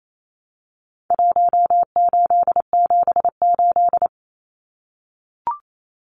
Morse code, heard as a keyed tone, about 28 wpm, 700 Hz. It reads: 1878
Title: Numbers - Contest Style - 4-Digit 28wpm